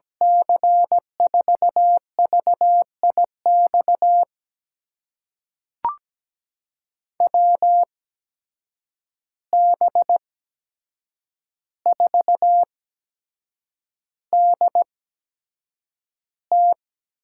Morse code, heard as a keyed tone, about 17 wpm, spaced as for 7 wpm, 700 Hz. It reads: C4VIX WB4DT
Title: USA Call Signs - 8x Character Spacing - 17wpm